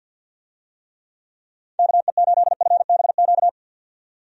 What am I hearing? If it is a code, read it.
Morse code, heard as a keyed tone, about 50 wpm, 700 Hz. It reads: KE8FBQ
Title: USA Call Signs - 6 Characters - 50wpm